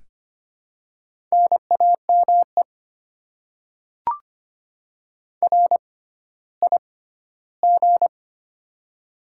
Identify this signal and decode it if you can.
Morse code, heard as a keyed tone, about 25 wpm, spaced as for 14 wpm, 700 Hz. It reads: NAME RIG
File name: Top 20 QSO Elements - 6x Character Spacing - 25wpm